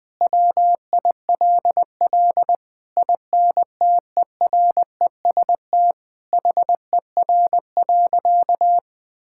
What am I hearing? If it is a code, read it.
Morse code, heard as a keyed tone, about 20 wpm, 700 Hz. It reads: WILL INTEREST HER.
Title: Sentences from Top 1000 Words - Review Part 5 - 20wpm